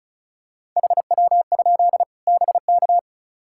Morse code, heard as a keyed tone, about 36 wpm, 700 Hz. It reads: HW? BK
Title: Practice QSO - Exchange 1 - Beginning - 35wpm